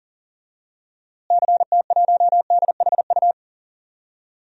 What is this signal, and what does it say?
Morse code, heard as a keyed tone, about 40 wpm, 700 Hz. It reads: CT1DHU
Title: DXCC-100 Call Signs - 6 Characters - Repeated 3x - 40wpm